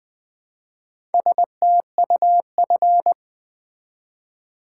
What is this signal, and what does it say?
Morse code, heard as a keyed tone, about 20 wpm, 700 Hz. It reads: STUF
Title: Build A Word - 1 Letter At A Time - 3 to 5-Letter Words - 20wpm